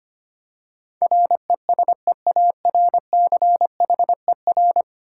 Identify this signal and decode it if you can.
Morse code, heard as a keyed tone, about 25 wpm, 700 Hz. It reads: RESEARCHER
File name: Ten Letter Words - 25wpm